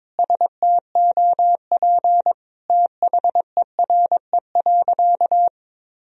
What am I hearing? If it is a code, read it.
Morse code, heard as a keyed tone, about 22 wpm, 700 Hz. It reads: STOP THERE.
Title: Sentences from Top 1000 Words - Review Part 1 - 22wpm